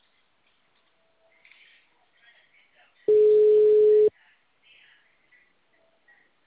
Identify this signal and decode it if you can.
Ringback tone (Europe, 425 Hz).